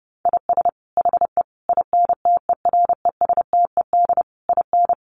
Morse code, heard as a keyed tone, about 30 wpm, 700 Hz. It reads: IS HE INTERESTED IN
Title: Sentences from Top 900 Words 30wpm